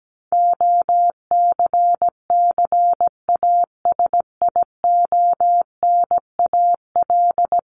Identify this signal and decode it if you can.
Morse code, heard as a keyed tone, about 17 wpm, 700 Hz. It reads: OCCASIONAL
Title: LY Suffix - Top 100 Words with Base Words 17wpm